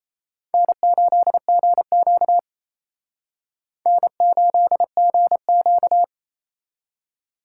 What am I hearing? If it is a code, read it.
Morse code, heard as a keyed tone, about 34 wpm, 700 Hz. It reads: N8GQ N8GQ
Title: Speed Racing - USA 1x2 Call Signs 28wpm